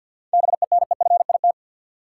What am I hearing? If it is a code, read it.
Morse code, heard as a keyed tone, about 53 wpm, 700 Hz. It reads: BENEFIT
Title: Sentences from Top 1000 Words - Review Part 6 - New Word Drill - 50wpm